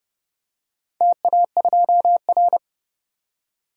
Morse code, heard as a keyed tone, about 30 wpm, 700 Hz. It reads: TA2R
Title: DXCC-100 Call Signs - 4 Characters - 30wpm